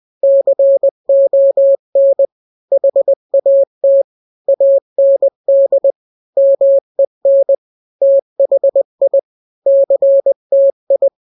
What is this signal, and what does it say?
Morse code, heard as a keyed tone, about 20 wpm, 550 Hz. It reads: CON HAT AND MEN THI CTI
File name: n-grams / 3-grams 20 18 WPM